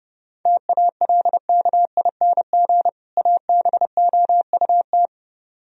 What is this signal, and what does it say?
Morse code, heard as a keyed tone, about 30 wpm, 700 Hz. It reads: TALKING ABOUT